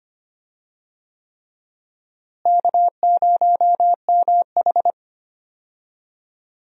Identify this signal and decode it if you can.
Morse code, heard as a keyed tone, about 25 wpm, 700 Hz. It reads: K0MH
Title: USA Call Signs - 4 Characters - Repeated 3x - 25wpm